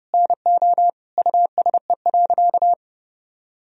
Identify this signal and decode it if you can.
Morse code, heard as a keyed tone, about 30 wpm, 700 Hz. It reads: NO USE.
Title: Sentences from Top 1000 Words - Review Part 4 - 30wpm